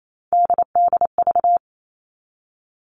Morse code, heard as a keyed tone, about 28 wpm, 700 Hz. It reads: DDV
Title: All Cut Numbers - 3-Digit 28wpm